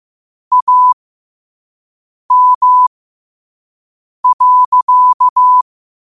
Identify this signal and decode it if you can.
Morse code, heard as a keyed tone, about 15 wpm, spaced as for 5 wpm, 1000 Hz. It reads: AM.